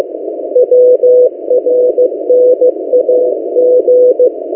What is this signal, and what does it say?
Morse code, heard as a keyed tone, about 15 wpm, 500 Hz. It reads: WRNAG
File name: M08a- -1355utc-8096khz